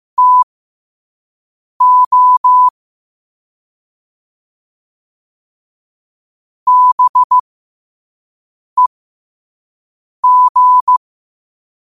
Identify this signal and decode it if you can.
Morse code, heard as a keyed tone, about 15 wpm, spaced as for 5 wpm, 1000 Hz. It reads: TO BEG
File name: qotdfiles / 5wpm